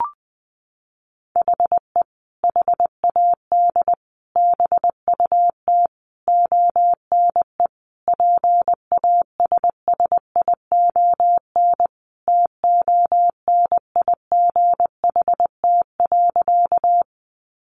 Morse code, heard as a keyed tone, about 20 wpm, 700 Hz. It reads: HE HAD BUT ONE PASSION TONIGHT.
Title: Sentences from Top 1300 Words 20wpm